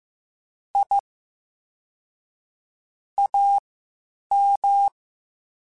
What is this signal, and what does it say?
Morse code, heard as a keyed tone, about 15 wpm, spaced as for 8 wpm, 800 Hz. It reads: I AM
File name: QSOs / QSOs-mp3s / WD0BGP 1@10